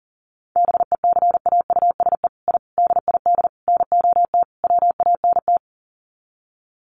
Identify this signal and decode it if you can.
Morse code, heard as a keyed tone, about 40 wpm, 700 Hz. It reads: BECAUSE I DID NOT WANT
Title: Sets of 5 Words Encore2 40wpm